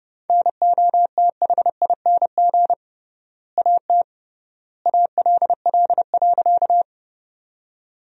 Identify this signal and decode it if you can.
Morse code, heard as a keyed tone, about 30 wpm, 700 Hz. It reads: NOTHING AT ALL.
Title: 3x Word-Spacing - Sentences from Top 500 Words - 30wpm